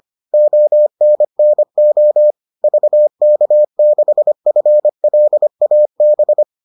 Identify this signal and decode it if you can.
Morse code, heard as a keyed tone, about 25 wpm, 600 Hz. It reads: ONNO VK6FLAB